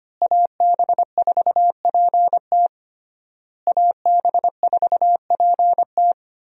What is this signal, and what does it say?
Morse code, heard as a keyed tone, about 25 wpm, 700 Hz. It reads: AB4PT AB4PT
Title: USA Call Signs - 5 Characters - Repeated 3x - 25wpm